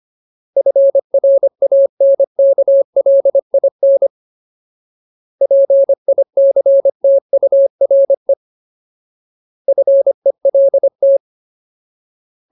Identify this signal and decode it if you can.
Morse code, heard as a keyed tone, about 25 wpm, 550 Hz. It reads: FRANKLIN PICTURE FELT
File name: Ultimate Word List / 25WPM Ultimate